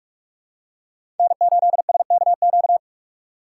Morse code, heard as a keyed tone, about 42 wpm, 700 Hz. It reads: N8SKQ
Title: USA Call Signs - 2x Character Spacing - 45wpm